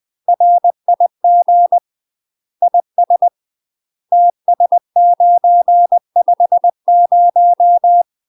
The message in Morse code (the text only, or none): RIG IS TS950